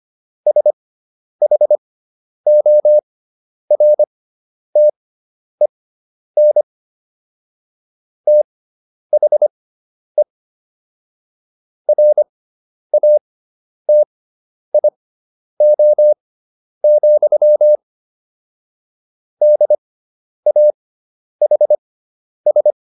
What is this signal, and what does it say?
Morse code, heard as a keyed tone, about 25 wpm, spaced as for 10 wpm, 600 Hz. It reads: SHORTEN THE RATIO, DAHS